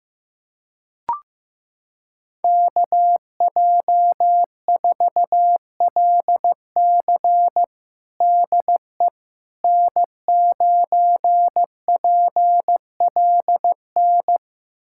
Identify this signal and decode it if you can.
Morse code, heard as a keyed tone, about 16 wpm, 700 Hz. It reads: KJ4LC DE N9PLN
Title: USA Callsign DE Callsign - 5 Characters - 15wpm